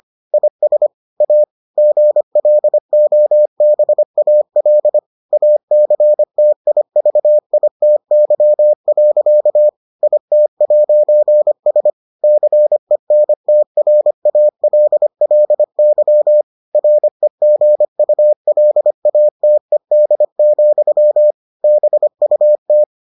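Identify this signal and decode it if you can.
Morse code, heard as a keyed tone, about 25 wpm, 600 Hz. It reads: IS A GLOBAL ACTIVITY. IT'S CENTRALLY REGULATED, BUT
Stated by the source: Morse edition of the Foundations of Amateur Radio episodes.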